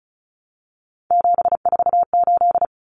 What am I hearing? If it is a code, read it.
Morse code, heard as a keyed tone, about 34 wpm, 700 Hz. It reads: 748
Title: Common Cut Numbers - 3-Digit 35wpm